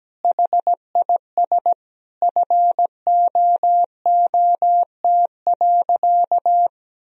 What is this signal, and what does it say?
Morse code, heard as a keyed tone, about 17 wpm, 700 Hz. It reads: HIS FOOT.
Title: Sentences from Top 1000 Words - Review Part 3 - 17wpm